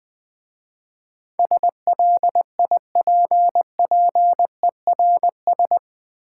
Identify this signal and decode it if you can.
Morse code, heard as a keyed tone, about 20 wpm, 700 Hz. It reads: SLIPPERS
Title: Easy Words - Speed-Racing with Hints - 20wpm